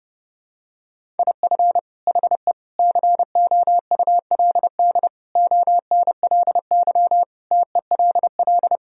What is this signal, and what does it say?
Morse code, heard as a keyed tone, about 32 wpm, 700 Hz. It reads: IF HE COULD ONLY TELL
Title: Sentences from Top 1000 Words - Review Part 1 - 30wpm